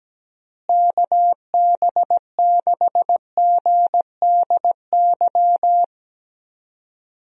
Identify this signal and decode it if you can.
Morse code, heard as a keyed tone, about 17 wpm, 700 Hz. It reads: KB6GDY
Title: USA Call Signs - 6 Characters - Repeated 3x - 17wpm